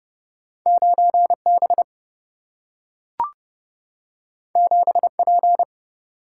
Morse code, heard as a keyed tone, about 30 wpm, 700 Hz. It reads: 9 B 7 P
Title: Single-Letter Single-Number 30wpm